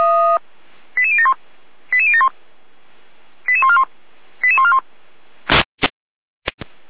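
Five-tone selective call (ZVEI), heard as quick runs of five tones, tones of about 70 ms. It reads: 88071 88071 88141 88141
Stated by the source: Five-tone ZVEI selective call 88149 received by SDR, logged by fire brigade Feuerwehr Orlamünde.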